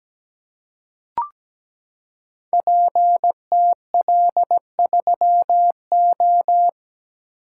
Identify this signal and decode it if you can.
Morse code, heard as a keyed tone, about 17 wpm, 700 Hz. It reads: PTL3O
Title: Code Groups - 5 Letter:Numbers - 17wpm